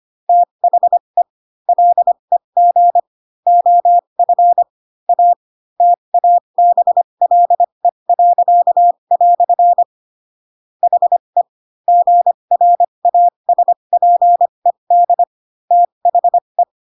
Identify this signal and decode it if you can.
Morse code, heard as a keyed tone, about 25 wpm, spaced as for 22 wpm, 700 Hz. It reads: THE LEG OF A TABLE." HE GRASPED THE